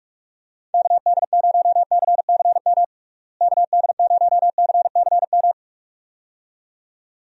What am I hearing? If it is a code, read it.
Morse code, heard as a keyed tone, about 46 wpm, 700 Hz. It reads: KD0CCM KD0CCM
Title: USA Call Signs - 6 Characters - Repeated 2x - 45wpm